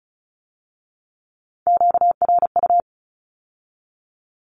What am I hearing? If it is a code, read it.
Morse code, heard as a keyed tone, about 36 wpm, 700 Hz. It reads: QRU
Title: Q-Codes with Spoken Meaning - 35wpm